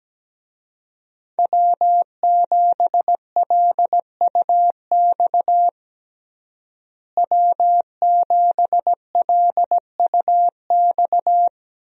Morse code, heard as a keyed tone, about 17 wpm, 700 Hz. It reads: W7LUX W7LUX
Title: USA Call Signs - 5 Characters - Repeated 3x - 17wpm